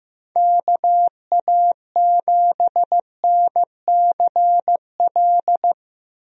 Morse code, heard as a keyed tone, about 15 wpm, 700 Hz. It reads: KA7NCL
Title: USA Call Signs - 6 Characters - Repeated 3x - 15wpm